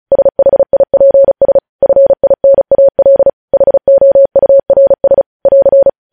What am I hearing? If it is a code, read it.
Morse code, heard as a keyed tone, about 36 wpm, 550 Hz. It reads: SHIPS FINAL HOURS <AR>
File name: podcast / News headlines-1x-35x35